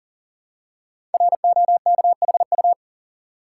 Morse code, heard as a keyed tone, about 42 wpm, 690 Hz. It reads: ROKHU
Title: Code Groups - 5 Letters - 40wpm